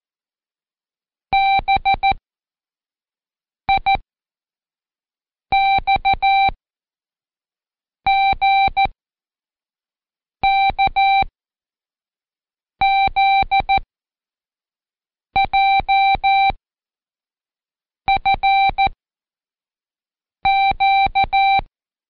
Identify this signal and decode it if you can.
Morse code, heard as a keyed tone, about 14 wpm, 780 Hz. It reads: B I X G K Z J F Q
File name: Code Quick / AudioDownld1 / Seg59-Day7-CheckTest 7